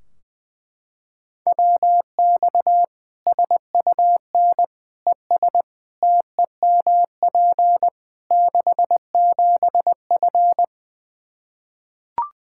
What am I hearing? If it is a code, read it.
Morse code, heard as a keyed tone, about 20 wpm, 700 Hz. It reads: WX SUN ES TEMP 67F
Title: Practice QSO - Exchange 2 - Info - 20wpm